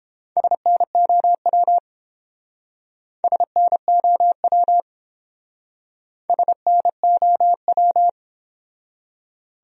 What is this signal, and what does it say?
Morse code, heard as a keyed tone, about 29 wpm, 700 Hz. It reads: SNOW SNOW SNOW